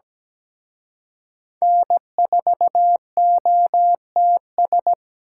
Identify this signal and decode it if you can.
Morse code, heard as a keyed tone, about 20 wpm, 700 Hz. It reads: N4OTS
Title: Speed Racing - USA 1x3 Call Signs 17wpm